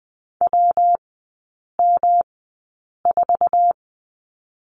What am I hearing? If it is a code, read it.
Morse code, heard as a keyed tone, about 20 wpm, spaced as for 8 wpm, 700 Hz. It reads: WM4
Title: 152 Lesson 25e - Letter p - Three Characters 20wpm